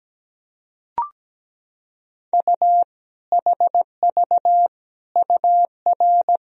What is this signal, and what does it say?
Morse code, heard as a keyed tone, about 17 wpm, 700 Hz. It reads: U HV UR
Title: Sentences from Top 400 Words and Abbreviations 17wpm